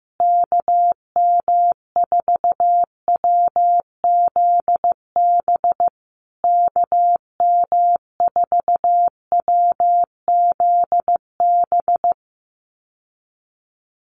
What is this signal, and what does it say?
Morse code, heard as a keyed tone, about 15 wpm, 700 Hz. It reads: KM4WZB KM4WZB
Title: USA Call Signs Repeated 3x 15wpm